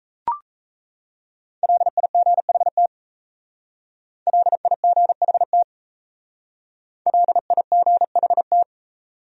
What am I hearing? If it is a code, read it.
Morse code, heard as a keyed tone, about 36 wpm, 700 Hz. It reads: LIGHT LIGHT LIGHT